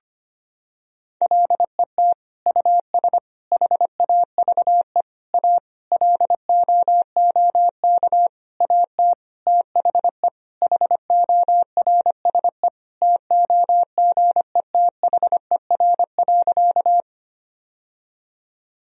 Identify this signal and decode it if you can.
Morse code, heard as a keyed tone, about 25 wpm, 700 Hz. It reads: LET US HAVE A LOOK AT THE HORSE TOGETHER.
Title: Sentences from Top 1000 Words - Review Part 1 - 25wpm